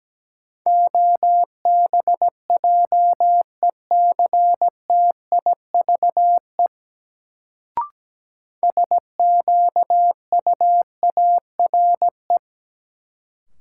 Morse code, heard as a keyed tone, about 17 wpm, 700 Hz. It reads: OBJECTIVE SQUARE
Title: Sentences from Top 1200 Words - New Word Drill Alt - 17wpm